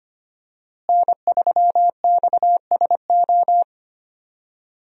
Morse code, heard as a keyed tone, about 25 wpm, 700 Hz. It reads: N3XSO
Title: USA Call Signs - 5 Characters - Repeated 3x - 25wpm